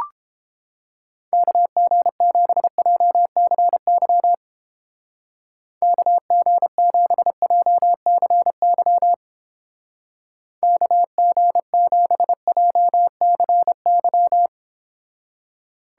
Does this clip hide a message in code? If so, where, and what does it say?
Yes: Morse code from 1.3 to 14.5 s, heard as a keyed tone, about 29 wpm, 700 Hz. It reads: KG7JCY KG7JCY KG7JCY